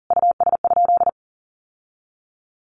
Morse code, heard as a keyed tone, about 40 wpm, 700 Hz. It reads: US?
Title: Sentences from Top 1500 Words 40wpm